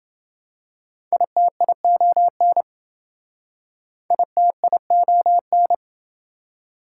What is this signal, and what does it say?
Morse code, heard as a keyed tone, about 27 wpm, 700 Hz. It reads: ITION ITION